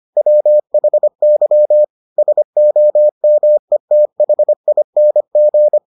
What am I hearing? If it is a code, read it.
Morse code, heard as a keyed tone, about 25 wpm, 600 Hz. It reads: WHY SOMETHING